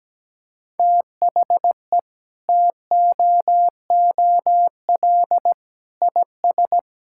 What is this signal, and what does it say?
Morse code, heard as a keyed tone, about 17 wpm, 700 Hz. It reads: THE TOOL IS
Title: Sentences from Top 1000 Words - Review Part 6 - 17wpm